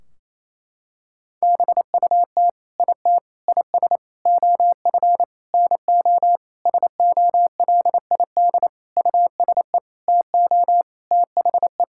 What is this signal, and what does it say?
Morse code, heard as a keyed tone, about 28 wpm, 700 Hz. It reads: BUT IT IS OF NO SOLID USE TO THE
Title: Sentences from Top 1600 Words 28wpm